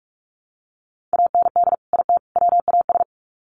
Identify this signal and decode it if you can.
Morse code, heard as a keyed tone, about 44 wpm, 700 Hz. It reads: AND IT WAS
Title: Sets of 3 Words 45wpm